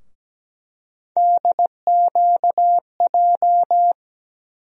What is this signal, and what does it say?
Morse code, heard as a keyed tone, about 17 wpm, 700 Hz. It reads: DQJ
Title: USA Call Sign Postfixes - 3 Letters 17wpm